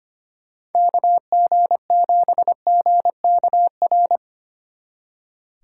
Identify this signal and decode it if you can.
Morse code, heard as a keyed tone, about 32 wpm, 700 Hz. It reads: KG7GKR